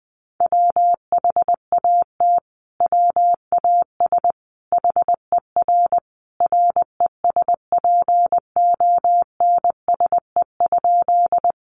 Morse code, heard as a keyed tone, about 20 wpm, 700 Hz. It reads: WHAT WAS HER RESPONSE?
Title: Sentences from Top 1400 Words 20wpm